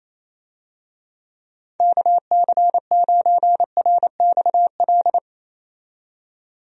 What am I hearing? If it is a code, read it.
Morse code, heard as a keyed tone, about 28 wpm, 700 Hz. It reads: KC9RXL
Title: USA Call Signs - 6 Characters - 28wpm